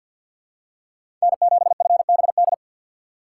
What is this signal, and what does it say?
Morse code, heard as a keyed tone, about 52 wpm, 700 Hz. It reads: N7FBD
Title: USA Call Signs - 5 Characters - 3x Character Spacing - 50wpm